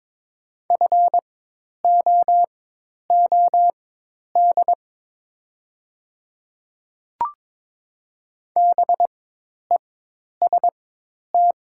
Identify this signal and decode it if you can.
Morse code, heard as a keyed tone, about 23 wpm, spaced as for 10 wpm, 700 Hz. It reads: FOOD BEST
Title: 4x Character Spacing - Top 500 Words - 22wpm